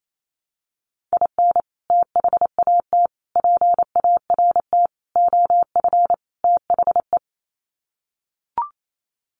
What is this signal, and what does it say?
Morse code, heard as a keyed tone, about 28 wpm, 700 Hz. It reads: IN THAT PART OF THE
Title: Sets of 5 Words Encore2 28wpm